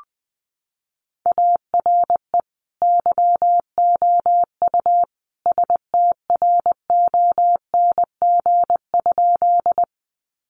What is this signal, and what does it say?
Morse code, heard as a keyed tone, about 20 wpm, 700 Hz. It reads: ARE YOU STRONG?
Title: Sentences from Top 500 Words 20wpm